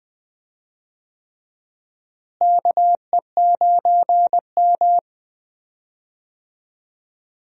Morse code, heard as a keyed tone, about 20 wpm, 700 Hz. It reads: KE9M
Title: USA Call Signs - 4 Characters - Repeated 3x - 20wpm